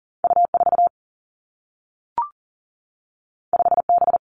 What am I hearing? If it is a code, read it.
Morse code, heard as a keyed tone, about 43 wpm, 700 Hz. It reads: U 4 5 B
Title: Two Challenging Letter-Number Combinations 40wpm